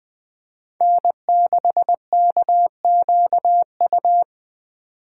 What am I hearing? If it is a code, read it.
Morse code, heard as a keyed tone, about 21 wpm, 700 Hz. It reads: N6KQU
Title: Speed Racing - USA Call Signs 17wpm